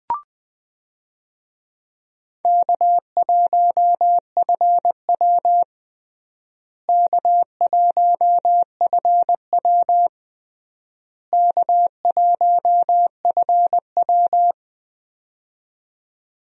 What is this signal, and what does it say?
Morse code, heard as a keyed tone, about 21 wpm, 700 Hz. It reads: K1FW K1FW K1FW
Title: USA Call Signs - 4 Characters - Repeated 3x - 20wpm